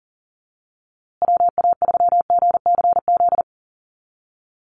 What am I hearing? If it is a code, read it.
Morse code, heard as a keyed tone, about 40 wpm, 700 Hz. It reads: WA3GCZ
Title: USA Call Signs Repeated 3x 40wpm